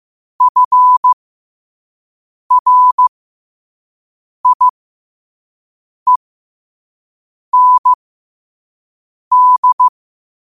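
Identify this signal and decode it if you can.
Morse code, heard as a keyed tone, about 15 wpm, spaced as for 5 wpm, 1000 Hz. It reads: FRIEND
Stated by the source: Morse code at 5 words per minute.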